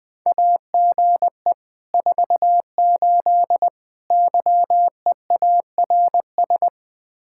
Morse code, heard as a keyed tone, about 20 wpm, 700 Hz. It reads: AGE 48 YEARS
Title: Speed-Racing Common QSO Phrases - 17wpm